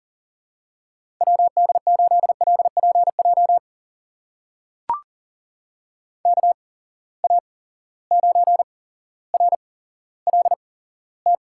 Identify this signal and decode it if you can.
Morse code, heard as a keyed tone, about 40 wpm, spaced as for 17 wpm, 700 Hz. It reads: WD8LPJ KA9RLT
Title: USA Call Signs - 6 Characters - 8x Character Spacing - 40wpm